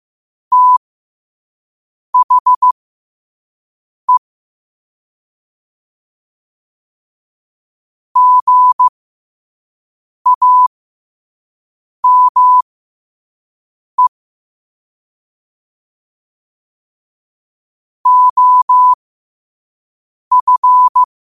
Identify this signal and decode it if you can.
Morse code, heard as a keyed tone, about 15 wpm, spaced as for 5 wpm, 1000 Hz. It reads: THE GAME OF
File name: qotdfiles / 5wpm